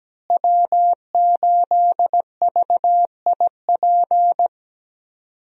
Morse code, heard as a keyed tone, about 17 wpm, 700 Hz. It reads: W8VIP
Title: USA Call Signs - 5 Characters - Repeated 2x - 17wpm